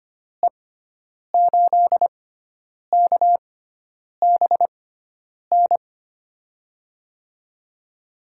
Morse code, heard as a keyed tone, about 25 wpm, spaced as for 9 wpm, 700 Hz. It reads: E8KBN
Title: USA Call Signs - 6 Characters - 6x Character Spacing - 25wpm